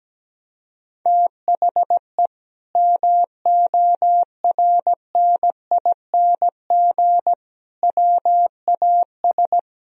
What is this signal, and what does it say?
Morse code, heard as a keyed tone, about 17 wpm, 700 Hz. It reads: THE MORNING WAS
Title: Sentences from Top 1000 Words - Review Part 3 - 17wpm